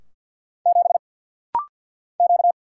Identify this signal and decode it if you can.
Morse code, heard as a keyed tone, about 50 wpm, 700 Hz. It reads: 7 Q
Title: Single Letter-Number - Rapid-Fire 50wpm